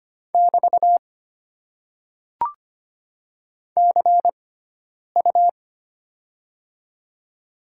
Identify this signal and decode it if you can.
Morse code, heard as a keyed tone, about 25 wpm, spaced as for 9 wpm, 700 Hz. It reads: <BT> CU
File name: Top 90 QSO Elements - 6x Character Spacing - 25wpm